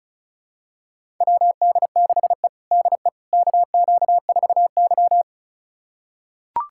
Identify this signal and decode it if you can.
Morse code, heard as a keyed tone, about 36 wpm, 700 Hz. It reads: WD6E DE KQ4Y
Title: USA Callsign DE Callsign - 4 Characters - 35wpm